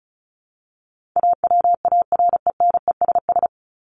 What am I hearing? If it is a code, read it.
Morse code, heard as a keyed tone, about 34 wpm, 700 Hz. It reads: AWARENESS
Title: Top 1501-1750 Words 35wpm